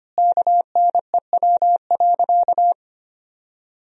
Morse code, heard as a keyed tone, about 25 wpm, 700 Hz. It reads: KNEW.
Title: 2x Word-Spacing - Sentences from Top 400 Words - 25wpm